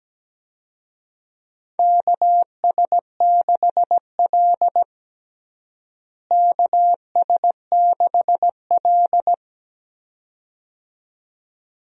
Morse code, heard as a keyed tone, about 17 wpm, 700 Hz. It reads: KS6L KS6L
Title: USA Call Signs - 4 Characters - Repeated 2x - 17wpm